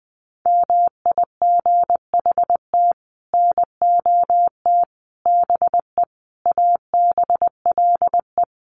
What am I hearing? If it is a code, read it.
Morse code, heard as a keyed tone, about 20 wpm, 700 Hz. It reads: MIGHT NOT BE ABLE
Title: Sets of 5 Words Encore2 20wpm